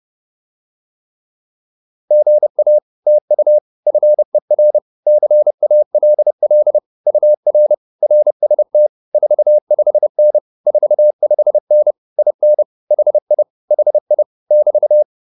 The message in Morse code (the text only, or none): GA TU FER CALL UR RST 45N 45N IN HI HI <BT>